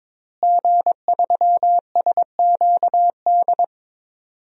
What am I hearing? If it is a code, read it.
Morse code, heard as a keyed tone, about 22 wpm, 700 Hz. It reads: G3SQD
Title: DXCC-100 Call Signs - 5 Characters - Repeated 3x - 22wpm